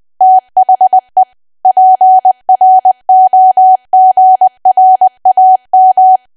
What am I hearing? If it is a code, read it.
Morse code, heard as a keyed tone, about 20 wpm, 750 Hz. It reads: THE PROGRAM
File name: Archive / 20 WPM / 20WPM